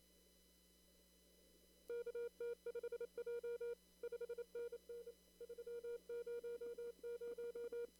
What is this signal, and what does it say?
Morse code, heard as a keyed tone, about 28 wpm, 470 Hz. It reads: KT5J 5NN 300
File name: CW / kt5j wpx cw-op-k5pi